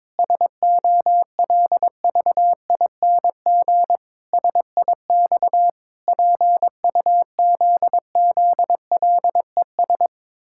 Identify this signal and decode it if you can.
Morse code, heard as a keyed tone, about 22 wpm, 700 Hz. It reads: SOLVING SIX PUZZLES
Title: Pangram Challenge v2 - 22wpm